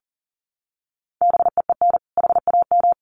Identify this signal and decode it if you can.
Morse code, heard as a keyed tone, about 40 wpm, 700 Hz. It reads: BEEN HAM
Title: Common QSO Phrases 40wpm